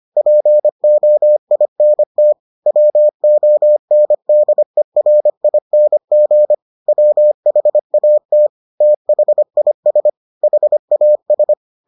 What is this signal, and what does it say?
Morse code, heard as a keyed tone, about 25 wpm, 600 Hz. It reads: POINT WONDERING WHAT THIS HAS